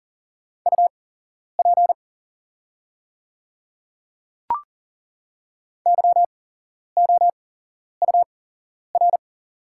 Morse code, heard as a keyed tone, about 40 wpm, spaced as for 11 wpm, 700 Hz. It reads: UP YOUR
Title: 8x Character Spacing - Top 100 Words - 40wpm